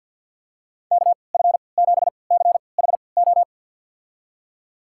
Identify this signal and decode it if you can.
Morse code, heard as a keyed tone, about 52 wpm, spaced as for 28 wpm, 700 Hz. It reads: KF7CHO
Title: USA Call Signs - 6 Characters - 3x Character Spacing - 50wpm